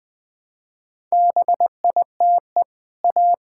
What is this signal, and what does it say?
Morse code, heard as a keyed tone, about 20 wpm, 700 Hz. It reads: BITE A
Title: Easy Sentences with Hints - 20wpm